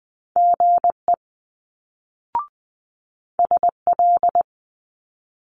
Morse code, heard as a keyed tone, about 20 wpm, 700 Hz. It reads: GE SL
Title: Multi-letter Phonemes Quick Fire 20wpm